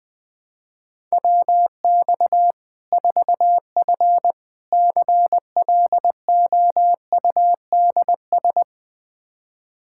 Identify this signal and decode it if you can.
Morse code, heard as a keyed tone, about 20 wpm, 700 Hz. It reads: WX 4F CLOUDS
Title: Practice QSO - Exchange 2 - Info - 20wpm